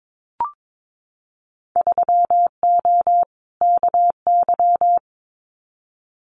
Morse code, heard as a keyed tone, about 22 wpm, 700 Hz. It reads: 3O KY
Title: Winter Field Day Exchanges 22wpm